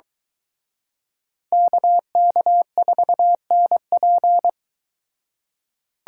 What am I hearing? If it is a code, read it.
Morse code, heard as a keyed tone, about 23 wpm, 700 Hz. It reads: KK4NP